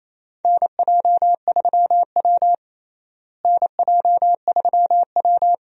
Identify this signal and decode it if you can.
Morse code, heard as a keyed tone, about 28 wpm, 700 Hz. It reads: NJ3W NJ3W
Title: USA Call Signs - 4 Characters - Repeated 2x - 28wpm